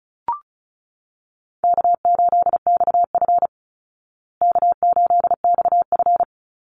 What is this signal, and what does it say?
Morse code, heard as a keyed tone, about 34 wpm, 700 Hz. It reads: K8XF K8XF
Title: Easy Call Signs Repeated 2x - 35wpm